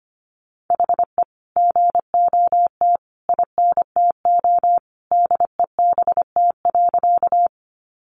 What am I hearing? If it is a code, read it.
Morse code, heard as a keyed tone, about 25 wpm, 700 Hz. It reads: HE GOT INTO DEBT.
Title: Sentences from Top 1300 Words 25wpm